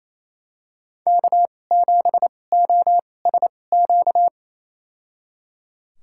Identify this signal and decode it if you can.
Morse code, heard as a keyed tone, about 28 wpm, spaced as for 20 wpm, 700 Hz. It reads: K7OSQ
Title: USA Call Signs - 5 Characters - 2x Character Spacing - 28wpm